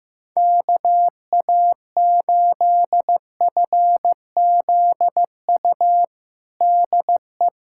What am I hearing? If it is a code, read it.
Morse code, heard as a keyed tone, about 15 wpm, 700 Hz. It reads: KA8FZU DE
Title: USA Callsign DE Callsign - 6 Characters - 15wpm